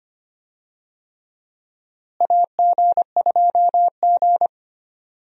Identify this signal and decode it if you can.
Morse code, heard as a keyed tone, about 25 wpm, 700 Hz. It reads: AG2G
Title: USA Call Signs - 4 Characters - Repeated 3x - 25wpm